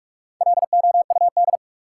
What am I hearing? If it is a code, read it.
Morse code, heard as a keyed tone, about 47 wpm, 700 Hz. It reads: LOUD
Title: Four Letter Words - 45wpm